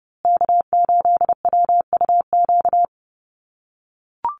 Morse code, heard as a keyed tone, about 30 wpm, 700 Hz. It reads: K8WUQ
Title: USA Call Signs Repeated 3x 30wpm